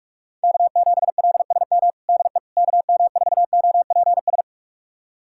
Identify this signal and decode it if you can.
Morse code, heard as a keyed tone, about 46 wpm, 700 Hz. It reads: K7LSM DE KM4OPS
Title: USA Callsign DE Callsign 45wpm